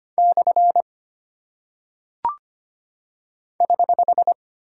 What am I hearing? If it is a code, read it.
Morse code, heard as a keyed tone, about 25 wpm, 700 Hz. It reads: / <HH>
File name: Single Punctuation-Prosign - 25wpm